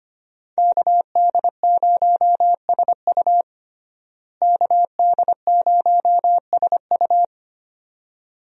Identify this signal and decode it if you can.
Morse code, heard as a keyed tone, about 25 wpm, 700 Hz. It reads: KD0SU KD0SU
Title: USA Call Signs - 5 Characters - Repeated 2x - 25wpm